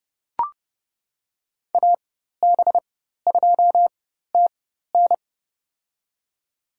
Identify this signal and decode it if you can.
Morse code, heard as a keyed tone, about 29 wpm, spaced as for 14 wpm, 700 Hz. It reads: AB2TN
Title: USA Call Signs - 5 Characters - 4x Character Spacing - 30wpm